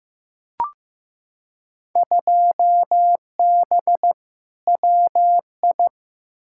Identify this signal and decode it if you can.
Morse code, heard as a keyed tone, about 15 wpm, 700 Hz. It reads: 2B WI
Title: Field Day Exchanges - 15wpm